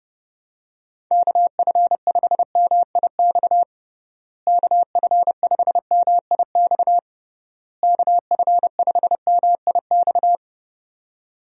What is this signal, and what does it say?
Morse code, heard as a keyed tone, about 32 wpm, 700 Hz. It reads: KF5MIX KF5MIX KF5MIX
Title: USA Call Signs - 6 Characters - Repeated 3x - 30wpm